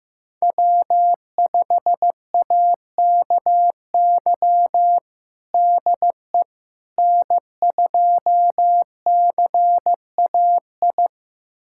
Morse code, heard as a keyed tone, about 15 wpm, 700 Hz. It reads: W5AKY DE N2CAI
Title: USA Callsign DE Callsign - 5 Characters - 15wpm